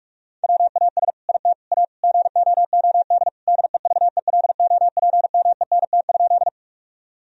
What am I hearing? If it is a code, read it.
Morse code, heard as a keyed tone, about 46 wpm, 700 Hz. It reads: WAS IT A GOOD DEVELOPMENT?
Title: Sentences from Top 1000 Words - Review Part 4 - 45wpm